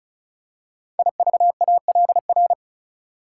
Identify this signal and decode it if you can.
Morse code, heard as a keyed tone, about 36 wpm, 700 Hz. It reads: IVALR
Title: Code Groups - 5 Letters - 35wpm